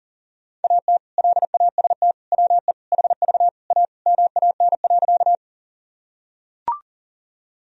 Morse code, heard as a keyed tone, about 40 wpm, 700 Hz. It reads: AT LAST WE HV A MAN.
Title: Sentences from Top 300 Words and Abbreviations 40wpm